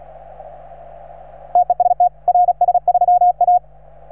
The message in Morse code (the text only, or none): TEST RS3A